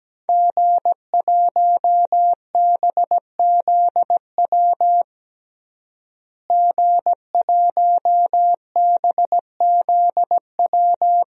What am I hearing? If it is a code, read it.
Morse code, heard as a keyed tone, about 17 wpm, 700 Hz. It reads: G1BZW G1BZW
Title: DXCC-100 Call Signs - 5 Characters - Repeated 3x - 17wpm